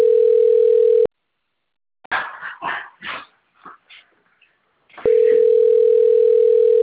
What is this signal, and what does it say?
Ringback tone (North America), 440 + 480 Hz.